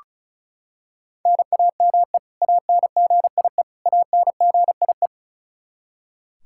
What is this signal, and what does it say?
Morse code, heard as a keyed tone, about 34 wpm, 700 Hz. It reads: NAME ANGIE ANGIE
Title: Practice QSO - Exchange 1 - Beginning - 35wpm